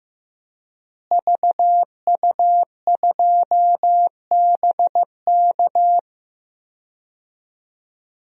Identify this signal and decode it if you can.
Morse code, heard as a keyed tone, about 15 wpm, 700 Hz. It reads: VU2BK
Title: DXCC-100 Call Signs - 5 Characters - Repeated 3x - 15wpm